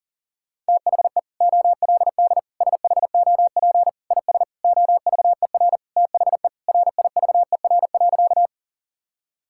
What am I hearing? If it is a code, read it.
Morse code, heard as a keyed tone, about 40 wpm, 700 Hz. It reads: THE OLD SHOP IS OVER THE RIVER.
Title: Sentences from Top 1600 Words 40wpm